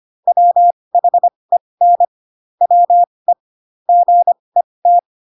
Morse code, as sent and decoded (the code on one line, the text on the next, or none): .-- .... -. / .-- . / --. -
WHEN WE GET